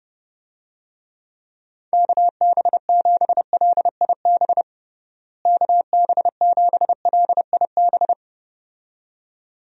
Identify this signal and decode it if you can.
Morse code, heard as a keyed tone, about 30 wpm, 700 Hz. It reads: KB7LIB KB7LIB
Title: USA Call Signs - 6 Characters - Repeated 2x - 30wpm